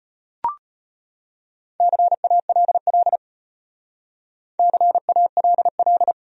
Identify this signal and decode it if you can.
Morse code, heard as a keyed tone, about 38 wpm, 700 Hz. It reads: CALL CALL